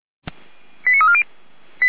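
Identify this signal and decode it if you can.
Five-tone selective call (ZVEI), heard as quick runs of five tones, tones of about 70 ms. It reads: 89340
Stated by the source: Five-tone ZVEI selective call received on SDR, from the feuerwehr-orlamuende.org fire brigade audio archive.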